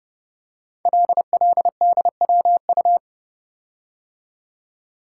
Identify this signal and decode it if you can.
Morse code, heard as a keyed tone, about 28 wpm, 700 Hz. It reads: LLDWU
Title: Code Groups - 5 Letter:Numbers - 30wpm